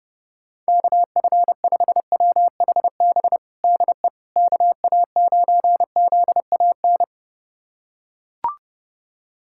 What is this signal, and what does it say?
Morse code, heard as a keyed tone, about 32 wpm, 700 Hz. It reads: KF5WHB DE KA9ZAN
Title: USA Callsign DE Callsign - 6 Characters - 30wpm